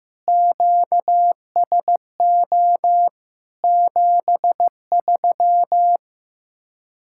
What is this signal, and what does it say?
Morse code, heard as a keyed tone, about 15 wpm, 700 Hz. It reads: QSO 73
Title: Speed-Racing Common QSO Phrases Encore - 15wpm